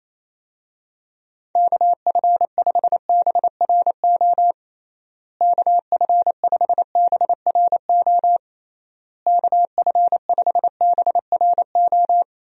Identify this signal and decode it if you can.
Morse code, heard as a keyed tone, about 28 wpm, 700 Hz. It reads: KF5BRO KF5BRO KF5BRO
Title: USA Call Signs - 6 Characters - Repeated 3x - 28wpm